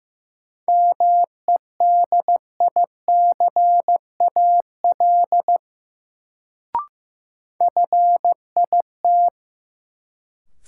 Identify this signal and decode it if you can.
Morse code, heard as a keyed tone, about 16 wpm, 700 Hz. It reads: MEDICAL FIT
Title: Sentences from Top 800 Words - New Word Drill - 15wpm